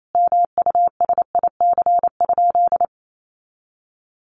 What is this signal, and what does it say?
Morse code, heard as a keyed tone, about 28 wpm, 700 Hz. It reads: MUSIC?
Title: Sentences from Top 300 Words 28wpm